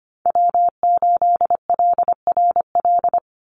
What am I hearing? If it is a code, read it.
Morse code, heard as a keyed tone, about 25 wpm, 700 Hz. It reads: W8LRL
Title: USA Call Signs 25wpm